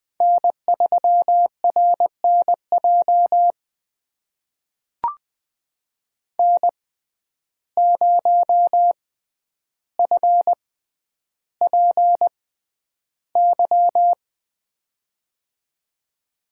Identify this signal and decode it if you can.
Morse code, heard as a keyed tone, about 20 wpm, spaced as for 10 wpm, 700 Hz. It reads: N3RNJ N0FPY
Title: USA Call Signs - 5 Characters - 6x Character Spacing - 20wpm